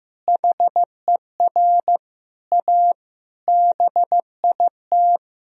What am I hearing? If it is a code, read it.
Morse code, heard as a keyed tone, about 16 wpm, 700 Hz. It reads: HER A BIT
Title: Sentences from Top 1000 Words - Review Part 2 - 15wpm